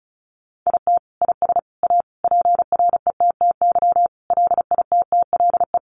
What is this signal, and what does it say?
Morse code, heard as a keyed tone, about 34 wpm, 700 Hz. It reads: IT IS A PRETTY LITTLE
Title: Sentences from Top 1000 Words 35wpm